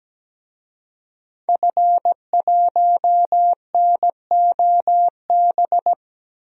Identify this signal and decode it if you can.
Morse code, heard as a keyed tone, about 17 wpm, 700 Hz. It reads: F1NOB
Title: DXCC-100 Call Signs - 5 Characters - Repeated 3x - 17wpm